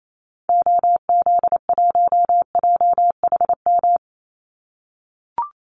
Morse code, heard as a keyed tone, about 28 wpm, 700 Hz. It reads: OZ1JHM
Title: Call Signs - 28wpm